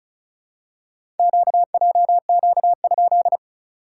Morse code, heard as a keyed tone, about 36 wpm, 700 Hz. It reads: QJQ?
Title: SOTA QSO Practice 35wpm